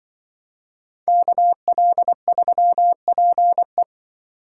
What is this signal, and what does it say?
Morse code, heard as a keyed tone, about 25 wpm, 700 Hz. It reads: KL3PE
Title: Speed Racing - USA 2x2 Call Signs 20wpm